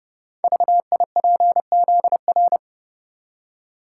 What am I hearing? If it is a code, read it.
Morse code, heard as a keyed tone, about 30 wpm, 700 Hz. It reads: VIPZR
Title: Code Groups - 5 Letters - 30wpm